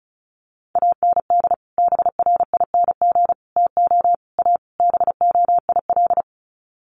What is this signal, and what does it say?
Morse code, heard as a keyed tone, about 34 wpm, 700 Hz. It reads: AND BRING TO A BOIL
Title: Sets of 5 Words Encore 35wpm